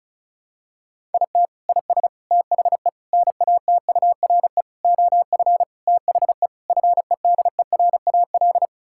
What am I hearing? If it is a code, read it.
Morse code, heard as a keyed tone, about 36 wpm, 700 Hz. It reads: IT IS THE NATURE OF THE FEDERAL
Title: Sentences from Top 1000 Words - Review Part 5 - 35wpm